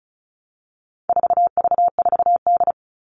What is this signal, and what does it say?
Morse code, heard as a keyed tone, about 34 wpm, 700 Hz. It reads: 4V4D
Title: All Cut Numbers - 4-Digit 35wpm